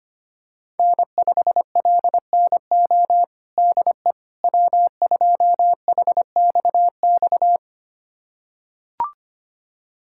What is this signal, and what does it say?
Morse code, heard as a keyed tone, about 25 wpm, 700 Hz. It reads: N5LNO DE W2HXX
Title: USA Callsign DE Callsign - 5 Characters - 25wpm